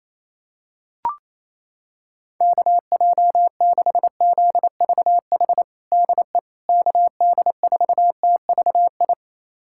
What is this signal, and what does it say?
Morse code, heard as a keyed tone, about 28 wpm, 700 Hz. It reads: KJ6ZVH DE KD4TVI
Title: USA Callsign DE Callsign 28wpm